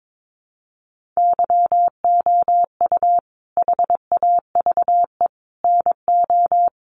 Morse code, heard as a keyed tone, about 22 wpm, 700 Hz. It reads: YOU HAVE NO
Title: Sentences from Top 1500 Words 22wpm